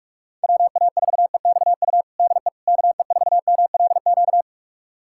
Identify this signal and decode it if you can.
Morse code, heard as a keyed tone, about 41 wpm, 700 Hz. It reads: WA4EXU DE KE4MLQ
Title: USA Callsign DE Callsign - 6 Characters - 45wpm